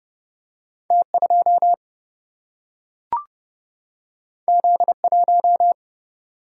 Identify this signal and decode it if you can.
Morse code, heard as a keyed tone, about 30 wpm, 700 Hz. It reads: T2 Z1
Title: Single-Letter Single-Number 30wpm